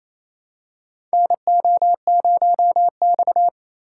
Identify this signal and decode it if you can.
Morse code, heard as a keyed tone, about 28 wpm, 700 Hz. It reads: NO0X
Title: USA Call Signs - 4 Characters - 2x Character Spacing - 28wpm